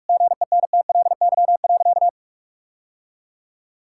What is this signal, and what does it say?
Morse code, heard as a keyed tone, about 41 wpm, 700 Hz. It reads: GENTLY.
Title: Sentences from Top 1600 Words 45wpm